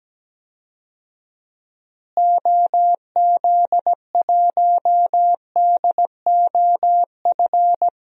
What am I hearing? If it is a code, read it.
Morse code, heard as a keyed tone, about 17 wpm, 700 Hz. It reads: OZ1DOF
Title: DXCC-100 Call Signs - 6 Characters - Repeated 2x - 17wpm